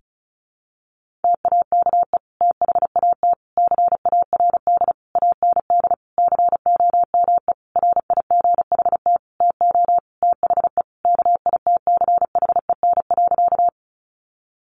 Morse code, heard as a keyed tone, about 34 wpm, 700 Hz. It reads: TAKE THAT CARD AND COME RIGHT TO THE KITCHEN.
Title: Sentences from Top 1400 Words 35wpm